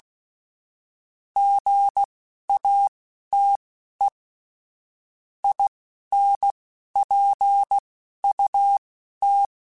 Morse code, heard as a keyed tone, about 16 wpm, spaced as for 11 wpm, 800 Hz. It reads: GATE INPUT
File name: Technical Subjects / Technical Subjects-mp3s / OR Gate @10